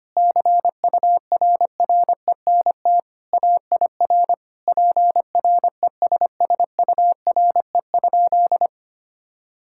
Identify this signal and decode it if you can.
Morse code, heard as a keyed tone, about 25 wpm, 700 Hz. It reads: CURRENT AIR PRESSURE?
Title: Sentences from Top 1000 Words - Review Part 6 - 25wpm